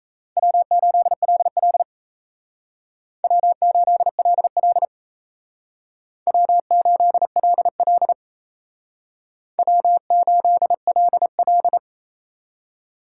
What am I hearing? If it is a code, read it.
Morse code, heard as a keyed tone, about 35 wpm, 700 Hz. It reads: W8LL W8LL W8LL W8LL
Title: Speed Racing - USA 1x2 Call Signs 28wpm